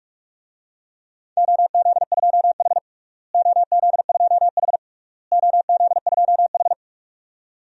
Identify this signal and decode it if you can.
Morse code, heard as a keyed tone, about 42 wpm, 700 Hz. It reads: OZ2H OZ2H OZ2H
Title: DXCC-100 Call Signs - 4 Characters - Repeated 3x - 45wpm